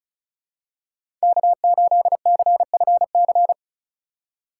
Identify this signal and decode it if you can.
Morse code, heard as a keyed tone, about 34 wpm, 700 Hz. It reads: K8CFC
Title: USA Call Signs - 5 Characters - 2x Character Spacing - 35wpm